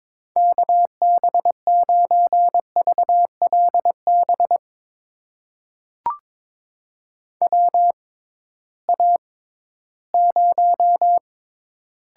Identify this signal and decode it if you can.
Morse code, heard as a keyed tone, about 21 wpm, spaced as for 14 wpm, 700 Hz. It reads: KB9VLB WA0
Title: USA Call Signs - 6x Character Spacing - 22wpm